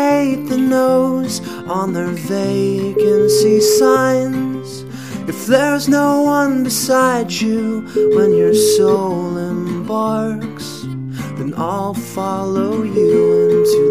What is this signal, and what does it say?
Ringback tone (Europe, 425 Hz), with noise.